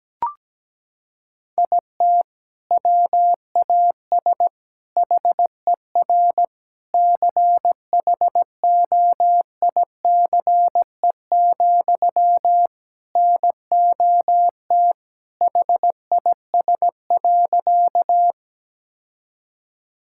Morse code, heard as a keyed tone, about 17 wpm, 700 Hz. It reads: IT WAS HER CHOICE, NOT HIS.
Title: Sentences from Top 1000 Words - Review Part 6 - 17wpm